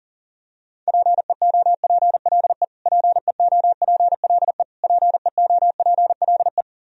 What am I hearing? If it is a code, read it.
Morse code, heard as a keyed tone, about 40 wpm, 700 Hz. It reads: PEOPLE PEOPLE PEOPLE
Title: Top 100 Words Repeated 40wpm